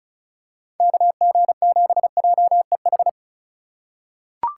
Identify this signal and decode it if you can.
Morse code, heard as a keyed tone, about 34 wpm, 700 Hz. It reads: KG7JEH
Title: USA Call Signs - 6 Characters - 3x Character Spacing - 35wpm